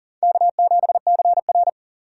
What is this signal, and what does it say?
Morse code, heard as a keyed tone, about 37 wpm, 700 Hz. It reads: K7CR
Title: USA Call Signs - 4 Characters - 8x Character Spacing - 40wpm